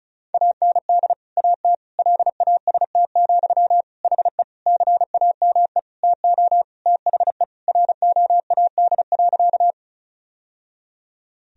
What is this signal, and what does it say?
Morse code, heard as a keyed tone, about 34 wpm, 700 Hz. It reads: AND AT LAST, HE CAME TO THE ROAD.
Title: Sentences from Top 1000 Words - Review Part 2 - 35wpm